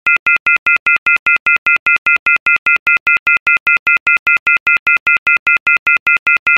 Off-hook warning tone, 1400 + 2060 + 2450 + 2600 Hz, with noise.
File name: Receiver Off - Hook Tone